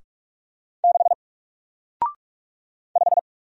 Morse code, heard as a keyed tone, about 44 wpm, 700 Hz. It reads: 6 5
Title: Single Character (Letter:Number:Punctuation:Prosign) Alt 45wpm